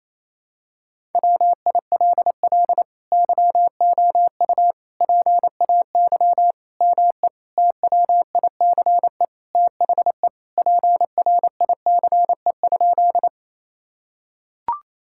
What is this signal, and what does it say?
Morse code, heard as a keyed tone, about 28 wpm, 700 Hz. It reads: WILL YOU PAY ME TWICE THE PRICE?
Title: Sentences from Top 1600 Words 28wpm